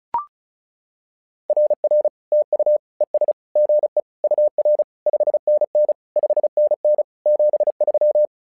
Morse code, heard as a keyed tone, about 34 wpm, 600 Hz. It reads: RR TU ES GE UR 5NN 5NN 73
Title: SOTA QSO Practice 35wpm